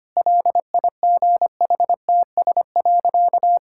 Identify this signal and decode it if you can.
Morse code, heard as a keyed tone, about 25 wpm, 700 Hz. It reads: LIGHTS.
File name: Easy Sentences with Hints - 25wpm